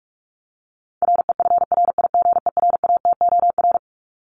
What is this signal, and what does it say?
Morse code, heard as a keyed tone, about 44 wpm, 700 Hz. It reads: REFRIGERATOR
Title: Top 1501-1750 Words 45wpm